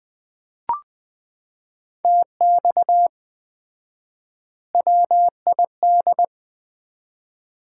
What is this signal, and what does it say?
Morse code, heard as a keyed tone, about 20 wpm, 700 Hz. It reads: TX WID
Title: Common QSO Phrases - 4x Word-Spacing - 20wpm